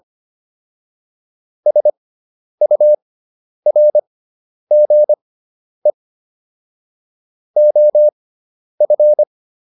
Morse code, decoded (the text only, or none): SURGE OF